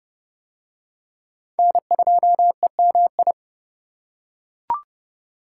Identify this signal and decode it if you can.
Morse code, heard as a keyed tone, about 29 wpm, 700 Hz. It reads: N2EMI